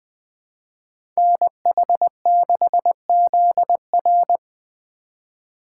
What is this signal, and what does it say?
Morse code, heard as a keyed tone, about 21 wpm, 700 Hz. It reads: NH6ZR
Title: USA Call Signs - 5 Characters - 4x Character Spacing - 20wpm